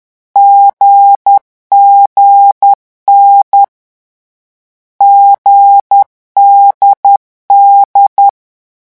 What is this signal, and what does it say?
Morse code, heard as a keyed tone, about 10 wpm, 800 Hz. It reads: GGN GDD
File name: morse mp3 / lecon4-vitesse 10